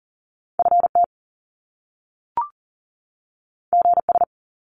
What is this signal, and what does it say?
Morse code, heard as a keyed tone, about 40 wpm, 700 Hz. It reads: FT GS
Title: Top Two Letter Combinations in Words 40wpm